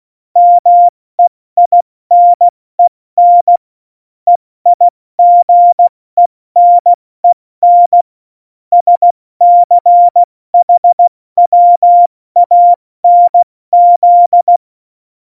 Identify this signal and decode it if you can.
Morse code, heard as a keyed tone, about 16 wpm, 700 Hz. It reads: MEINEN EIGENEN SCHWANZ
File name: Morsekurs / Witze / Witz 14. August 2018, 70 F80